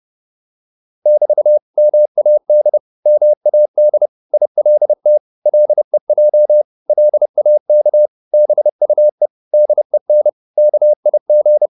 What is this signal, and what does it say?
Morse code, heard as a keyed tone, about 30 wpm, 600 Hz. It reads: <BT> MAD MAD ILT LEJ LAK BUE DEN KIG
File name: Korte ord DK 30wpm